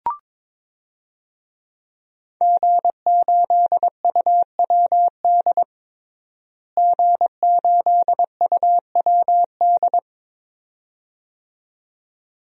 Morse code, heard as a keyed tone, about 22 wpm, 700 Hz. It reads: G8UWD G8UWD
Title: DXCC-100 Call Signs - 5 Characters - Repeated 2x - 22wpm